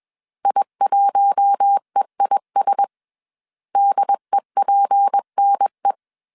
Morse code, heard as a keyed tone, about 21 wpm, 800 Hz. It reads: I1EIS DEPNE
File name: CW / DET / 1 6 1500C S B 20wpm 2048